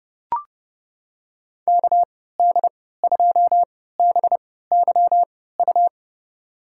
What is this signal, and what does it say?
Morse code, heard as a keyed tone, about 32 wpm, spaced as for 17 wpm, 700 Hz. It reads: KD2BYU
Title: USA Call Signs - 6 Characters - 3x Character Spacing - 30wpm